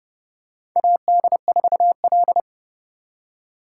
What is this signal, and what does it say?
Morse code, heard as a keyed tone, about 30 wpm, 700 Hz. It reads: AD4L
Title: USA Call Signs - 4x Character Spacing - 30wpm